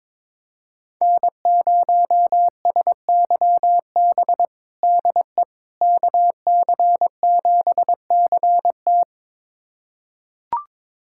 Morse code, heard as a keyed tone, about 22 wpm, 700 Hz. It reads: N0SYB DE KC7CT
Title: USA Callsign DE Callsign - 5 Characters - 22wpm